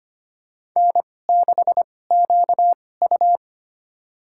Morse code, heard as a keyed tone, about 25 wpm, spaced as for 18 wpm, 700 Hz. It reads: N6QU
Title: USA Call Signs - 4 Characters - 2x Character Spacing - 25wpm